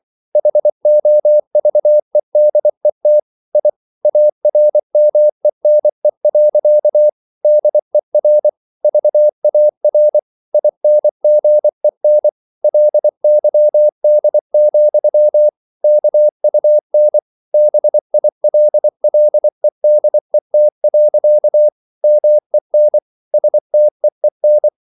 Morse code, heard as a keyed tone, about 24 wpm, 600 Hz. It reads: HOVEDET I ARMENE. DER VAR INGEN LYD, KUN BILLEDET. MEN STEEN
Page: Ferie 24 wpm